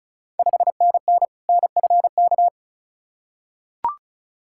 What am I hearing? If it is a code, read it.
Morse code, heard as a keyed tone, about 36 wpm, 690 Hz. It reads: 5NN NFK
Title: Ontario QSO Party Exchange 35wpm